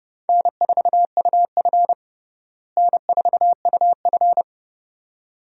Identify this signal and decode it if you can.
Morse code, heard as a keyed tone, about 30 wpm, 700 Hz. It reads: N4UF N4UF
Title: USA Call Signs - 4 Characters - Repeated 2x - 30wpm